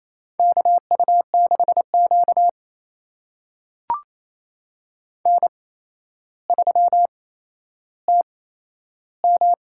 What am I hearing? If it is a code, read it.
Morse code, heard as a keyed tone, about 28 wpm, spaced as for 12 wpm, 700 Hz. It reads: KU6Q N3TM
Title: USA Call Signs - 4 Characters - 8x Character Spacing - 28wpm